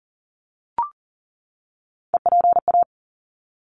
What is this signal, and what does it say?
Morse code, heard as a keyed tone, about 40 wpm, 700 Hz. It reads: EPA
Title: ARRL Sections 40wpm